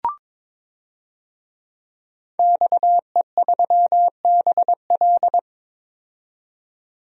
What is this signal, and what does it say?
Morse code, heard as a keyed tone, about 22 wpm, 700 Hz. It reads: XE3BL
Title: DXCC-100 Call Signs - 5 Characters - 22wpm